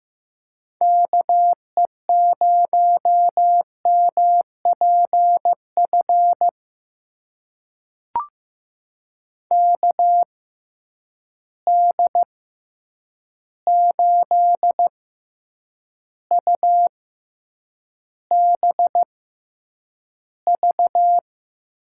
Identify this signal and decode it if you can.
Morse code, heard as a keyed tone, about 15 wpm, spaced as for 8 wpm, 700 Hz. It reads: KE0MPF KD8UBV
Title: USA Call Signs - 6 Characters - 6x Character Spacing - 15wpm